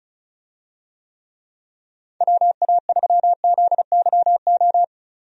Morse code, heard as a keyed tone, about 34 wpm, 700 Hz. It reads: WA3ZYO
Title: USA Call Signs - 6 Characters - 35wpm